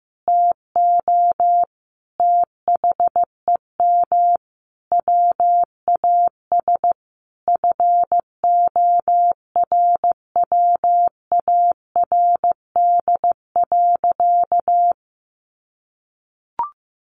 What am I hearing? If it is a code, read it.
Morse code, heard as a keyed tone, about 15 wpm, 700 Hz. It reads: TO THEM WAS FORWARD.
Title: Sentences from Top 1500 Words 15wpm